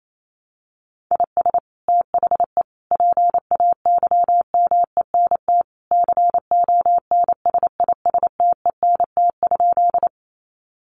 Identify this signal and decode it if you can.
Morse code, heard as a keyed tone, about 28 wpm, 700 Hz. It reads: IS THE PAYMENT CONSISTENT?
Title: Sentences from Top 1500 Words 28wpm